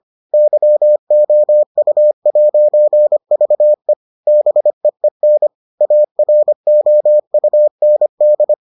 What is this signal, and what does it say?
Morse code, heard as a keyed tone, about 25 wpm, 600 Hz. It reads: YOU'VE BEEN AROUND